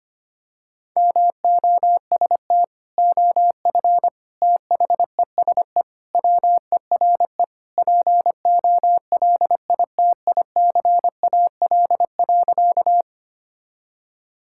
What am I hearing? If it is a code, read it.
Morse code, heard as a keyed tone, about 25 wpm, 700 Hz. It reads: MOST OF THESE WERE POLITICAL.
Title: Sentences from Top 1000 Words - Review Part 5 - 25wpm